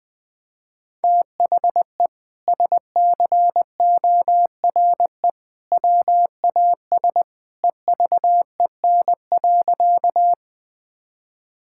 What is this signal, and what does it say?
Morse code, heard as a keyed tone, about 20 wpm, 700 Hz. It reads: THE SCORE WAS EVEN.
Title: Sentences from Top 1600 Words 20wpm